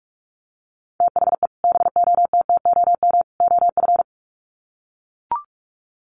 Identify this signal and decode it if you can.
Morse code, heard as a keyed tone, about 44 wpm, 700 Hz. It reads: THE BOTTOM OF
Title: Sets of 3 Words 45wpm